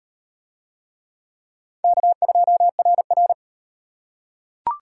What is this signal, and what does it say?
Morse code, heard as a keyed tone, about 37 wpm, 700 Hz. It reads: K2RR